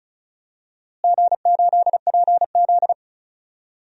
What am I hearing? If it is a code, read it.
Morse code, heard as a keyed tone, about 36 wpm, 700 Hz. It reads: G8PZ
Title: DXCC-100 Call Signs - 4 Characters - Repeated 2x - 35wpm